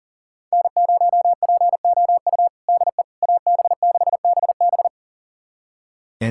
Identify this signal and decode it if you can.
Morse code, heard as a keyed tone, about 40 wpm, 700 Hz. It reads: N0POU DE AB6BB
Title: USA Callsign DE Callsign - 5 Characters - 40wpm